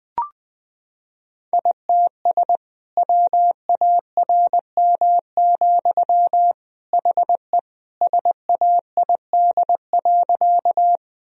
Morse code, heard as a keyed tone, about 20 wpm, 700 Hz. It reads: ITS WARM, HE SAID.
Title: Sentences from Top 1000 Words - Review Part 2 - 20wpm